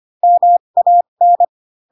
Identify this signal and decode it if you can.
Morse code, heard as a keyed tone, about 25 wpm, spaced as for 22 wpm, 700 Hz. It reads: MAN
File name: Pinocchio / Pinocchio ch2 18 25 ik1hge